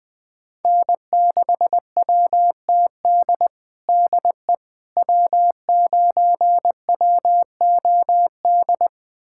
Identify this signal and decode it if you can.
Morse code, heard as a keyed tone, about 20 wpm, 700 Hz. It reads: N6WTD DE W9WOD
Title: USA Callsign DE Callsign - 5 Characters - 20wpm